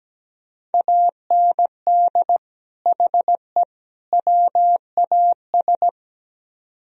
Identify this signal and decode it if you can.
Morse code, heard as a keyed tone, about 20 wpm, 700 Hz. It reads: AND HE WAS
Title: Speed-Racing - Sets of 3 Words 17wpm